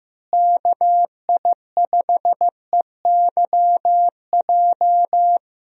Morse code, heard as a keyed tone, about 15 wpm, 700 Hz. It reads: KI5EYJ
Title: USA Call Signs - 6 Characters - Repeated 2x - 15wpm